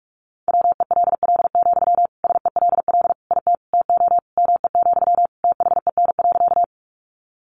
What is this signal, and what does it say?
Morse code, heard as a keyed tone, about 44 wpm, 700 Hz. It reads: WELL, SELL IT TO ME, THEN.
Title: Sentences from Top 1100 Words 45wpm